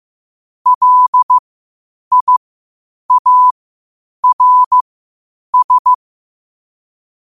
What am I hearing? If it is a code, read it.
Morse code, heard as a keyed tone, about 15 wpm, spaced as for 8 wpm, 1000 Hz. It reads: LIARS
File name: qotdfiles / 8wpm